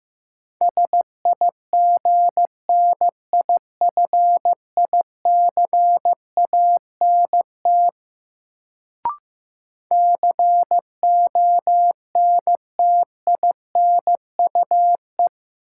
Morse code, heard as a keyed tone, about 15 wpm, 700 Hz. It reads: SIGNIFICANT CONTINUE
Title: Sentences from Top 1000 Words - Review Part 6 - New Word Drill - 15wpm